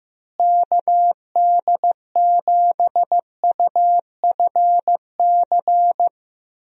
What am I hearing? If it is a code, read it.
Morse code, heard as a keyed tone, about 15 wpm, 700 Hz. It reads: KD7UFC
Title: USA Call Signs - 6 Characters - Repeated 3x - 15wpm